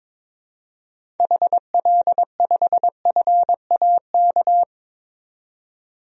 Morse code, heard as a keyed tone, about 22 wpm, 700 Hz. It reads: HL5FAK
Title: DXCC-100 Call Signs - 6 Characters - 22wpm